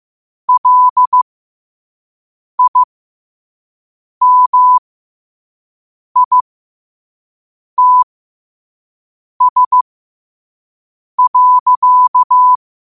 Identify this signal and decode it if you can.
Morse code, heard as a keyed tone, about 15 wpm, spaced as for 5 wpm, 1000 Hz. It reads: LIMITS.
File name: qotdfiles / 5wpm